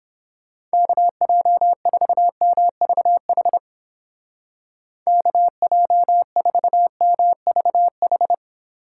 Morse code, heard as a keyed tone, about 29 wpm, 700 Hz. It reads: KJ4MVH KJ4MVH